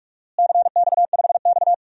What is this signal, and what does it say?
Morse code, heard as a keyed tone, about 46 wpm, 700 Hz. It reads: CX5X
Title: DXCC-100 Call Signs - 4 Characters - Repeated 2x - 45wpm